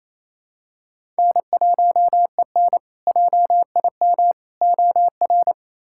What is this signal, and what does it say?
Morse code, heard as a keyed tone, about 28 wpm, 700 Hz. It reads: N1EN JIM OR
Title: K1USN SST QSO Practice - 28wpm